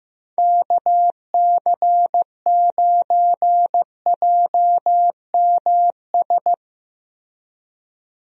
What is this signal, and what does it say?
Morse code, heard as a keyed tone, about 15 wpm, 700 Hz. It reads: KC9JMS
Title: USA Call Signs - 6 Characters - Repeated 2x - 15wpm